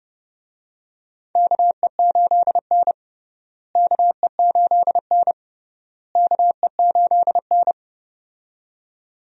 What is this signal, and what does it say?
Morse code, heard as a keyed tone, about 31 wpm, 700 Hz. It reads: KE8N KE8N KE8N
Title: USA Call Signs - 4 Characters - Repeated 3x - 30wpm